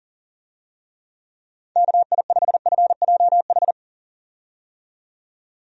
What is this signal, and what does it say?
Morse code, heard as a keyed tone, about 37 wpm, 700 Hz. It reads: KI5FJH
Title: USA Call Signs - 6 Characters - 40wpm